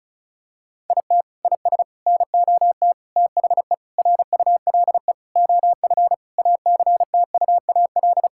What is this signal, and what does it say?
Morse code, heard as a keyed tone, about 34 wpm, 700 Hz. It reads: IT IS NOT THE RULE OF ACTUAL
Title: Sentences from Top 1000 Words - Review Part 5 - 35wpm